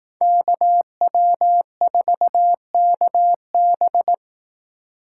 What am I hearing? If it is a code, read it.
Morse code, heard as a keyed tone, about 18 wpm, 700 Hz. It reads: KW4KB
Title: Speed Racing - USA Call Signs 15wpm